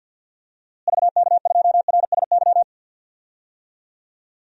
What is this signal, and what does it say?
Morse code, heard as a keyed tone, about 48 wpm, 700 Hz. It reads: VK2RSY
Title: DXCC-100 Call Signs - 6 Characters - Repeated 2x - 50wpm